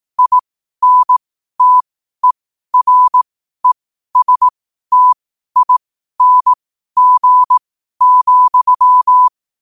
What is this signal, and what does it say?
Morse code, heard as a keyed tone, about 18 wpm, spaced as for 12 wpm, 1000 Hz. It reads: INTERESTING,
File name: qotdfiles / 12wpm